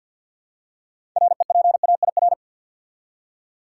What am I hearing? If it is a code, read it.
Morse code, heard as a keyed tone, about 52 wpm, 700 Hz. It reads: REPAIR
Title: Six Letter Words - 50wpm